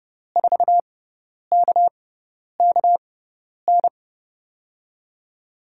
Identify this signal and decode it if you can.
Morse code, heard as a keyed tone, about 30 wpm, spaced as for 10 wpm, 700 Hz. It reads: 4KKN
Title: USA Call Signs - 6 Characters - 6x Character Spacing - 30wpm